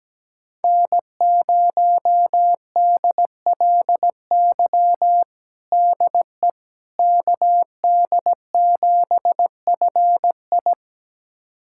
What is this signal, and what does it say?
Morse code, heard as a keyed tone, about 17 wpm, 700 Hz. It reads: N0DLY DE KD7FI
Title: USA Callsign DE Callsign - 5 Characters - 17wpm